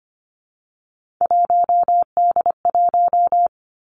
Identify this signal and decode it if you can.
Morse code, heard as a keyed tone, about 25 wpm, 700 Hz. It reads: 1D1
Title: Three Challenging Letter-Number Combinations 25wpm